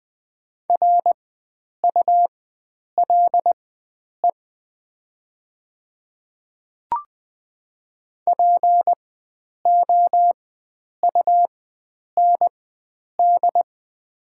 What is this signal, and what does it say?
Morse code, heard as a keyed tone, about 20 wpm, spaced as for 9 wpm, 690 Hz. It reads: RULE POUND
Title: 4x Character Spacing - Top 500 Words - 20wpm